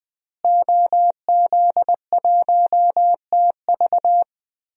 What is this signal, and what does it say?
Morse code, heard as a keyed tone, about 20 wpm, 700 Hz. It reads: OZ1TV
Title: DXCC-100 Call Signs - 5 Characters - Repeated 3x - 20wpm